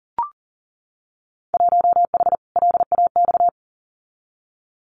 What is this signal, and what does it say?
Morse code, heard as a keyed tone, about 41 wpm, 700 Hz. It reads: 1H LAX
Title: Winter Field Day Exchanges 40wpm